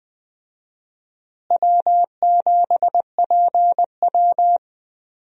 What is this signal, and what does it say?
Morse code, heard as a keyed tone, about 20 wpm, 700 Hz. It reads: W7PW
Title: USA Call Signs - 4 Characters - Repeated 3x - 20wpm